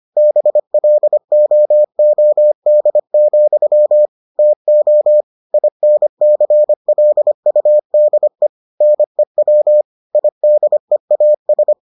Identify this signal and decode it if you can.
Morse code, heard as a keyed tone, about 25 wpm, 600 Hz. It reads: BLOOD, TO INCLUDE NEW IDEAS